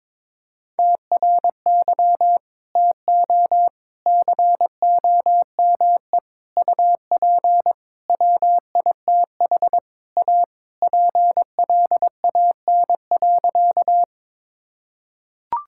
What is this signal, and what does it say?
Morse code, heard as a keyed tone, about 22 wpm, 700 Hz. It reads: TRY TO COME UP WITH A PLAN.
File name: practice / 22wpm / Sentences with 4-Word Phrases - 22wpm